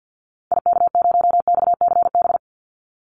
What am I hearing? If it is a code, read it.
Morse code, heard as a keyed tone, about 48 wpm, 700 Hz. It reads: IK0XCB
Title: Call Signs - 50wpm